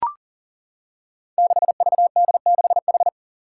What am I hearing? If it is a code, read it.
Morse code, heard as a keyed tone, about 40 wpm, 700 Hz. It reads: 6VD6H
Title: Five Challenging Letter-Number Combinations 40wpm